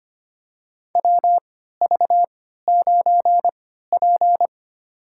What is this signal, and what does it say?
Morse code, heard as a keyed tone, about 26 wpm, spaced as for 14 wpm, 700 Hz. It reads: WV9P
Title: USA Call Signs - 4 Characters - 3x Character Spacing - 25wpm